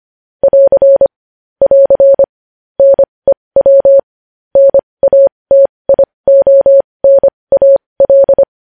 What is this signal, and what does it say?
Morse code, heard as a keyed tone, about 25 wpm, spaced as for 20 wpm, 550 Hz. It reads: <AR> <AR> NEW NATIONAL
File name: podcast / News headlines-1x-25x20